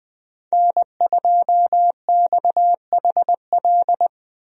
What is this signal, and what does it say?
Morse code, heard as a keyed tone, about 21 wpm, 700 Hz. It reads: N2XHL
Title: USA Call Signs - 5 Characters - Repeated 2x - 20wpm